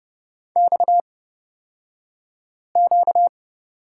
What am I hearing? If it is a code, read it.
Morse code, heard as a keyed tone, about 30 wpm, spaced as for 5 wpm, 700 Hz. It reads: XQ
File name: Lesson 38d - Letter q - Two Characters 30wpm